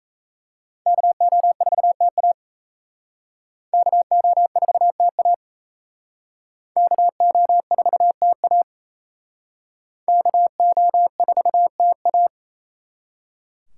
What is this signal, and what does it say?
Morse code, heard as a keyed tone, about 36 wpm, 700 Hz. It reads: KO4TA KO4TA KO4TA KO4TA